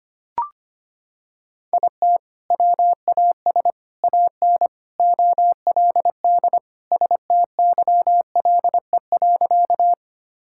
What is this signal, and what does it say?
Morse code, heard as a keyed tone, about 25 wpm, 700 Hz. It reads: IT WAS AN OLD STYLE.
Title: Sentences from Top 1000 Words - Review Part 5 - 25wpm